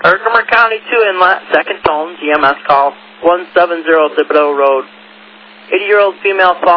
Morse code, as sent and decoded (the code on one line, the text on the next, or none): none